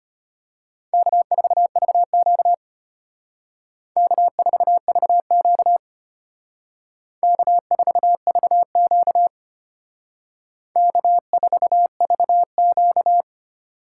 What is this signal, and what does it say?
Morse code, heard as a keyed tone, about 32 wpm, 700 Hz. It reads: K4VQ K4VQ K4VQ K4VQ